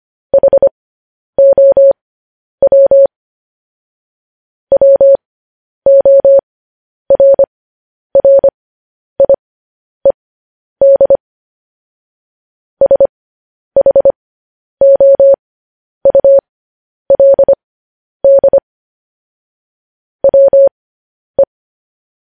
Morse code, heard as a keyed tone, about 25 wpm, spaced as for 10 wpm, 550 Hz. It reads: HOW WORRIED SHOULD WE
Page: news headlines in morse code at 10 wpm